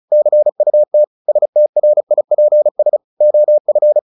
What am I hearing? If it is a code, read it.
Morse code, heard as a keyed tone, about 34 wpm, 600 Hz. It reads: CUT STRIPS OF
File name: the old man and the sea / the old man and the sea 1 35wpm